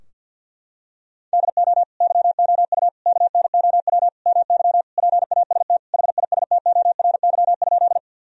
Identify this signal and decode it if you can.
Morse code, heard as a keyed tone, about 50 wpm, 700 Hz. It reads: DO YOU KNOW MY PAST HISTORY?
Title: Sentences from Top 1000 Words - Review Part 3 - 50wpm